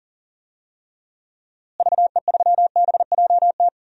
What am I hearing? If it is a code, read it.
Morse code, heard as a keyed tone, about 37 wpm, 700 Hz. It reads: VE3BJT
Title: DXCC-100 Call Signs - 6 Characters - Repeated 3x - 40wpm